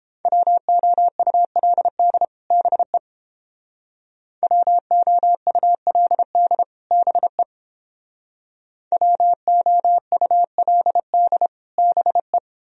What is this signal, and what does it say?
Morse code, heard as a keyed tone, about 29 wpm, 700 Hz. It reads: WOULD BE WOULD BE WOULD BE